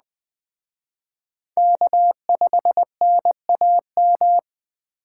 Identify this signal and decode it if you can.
Morse code, heard as a keyed tone, about 19 wpm, 700 Hz. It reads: K5NAM